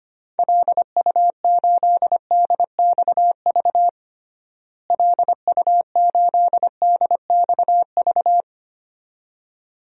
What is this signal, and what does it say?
Morse code, heard as a keyed tone, about 25 wpm, 700 Hz. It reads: LU8DXV LU8DXV
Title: DXCC-100 Call Signs - 6 Characters - Repeated 2x - 25wpm